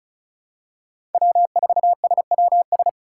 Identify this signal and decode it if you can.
Morse code, heard as a keyed tone, about 36 wpm, 700 Hz. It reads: W4SWS
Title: USA Call Signs - 4x Character Spacing - 35wpm